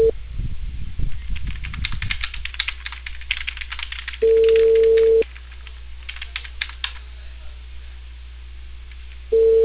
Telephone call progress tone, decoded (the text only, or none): ringback tone